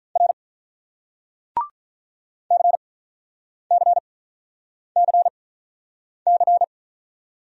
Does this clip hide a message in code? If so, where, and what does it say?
Yes: Morse code from 0.2 to 6.6 s, heard as a keyed tone, about 45 wpm, spaced as for 9 wpm, 700 Hz. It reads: R CCCC